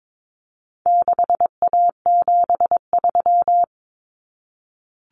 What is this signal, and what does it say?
Morse code, heard as a keyed tone, about 22 wpm, 700 Hz. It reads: 6A73
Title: Common Cut Numbers - 4-Digit 22wpm